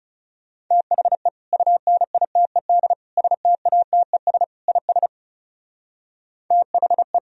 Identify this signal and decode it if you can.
Morse code, heard as a keyed tone, about 34 wpm, 700 Hz. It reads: THE UNITED STATES IS THE
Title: Speed-Racing - Sets of 4 Words 30wpm